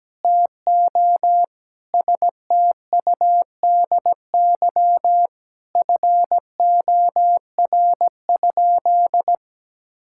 Morse code, heard as a keyed tone, about 17 wpm, 700 Hz. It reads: TO STUDY FOR?
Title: Sentences from Top 1000 Words - Review Part 6 - 17wpm